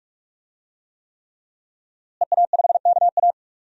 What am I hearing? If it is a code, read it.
Morse code, heard as a keyed tone, about 46 wpm, 700 Hz. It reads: EA5KA
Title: DXCC-100 Call Signs - 5 Characters - Repeated 3x - 45wpm